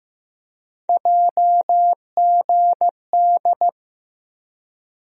Morse code, heard as a keyed tone, about 15 wpm, 700 Hz. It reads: JGD
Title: USA Call Sign Postfixes - 3 Letters 15wpm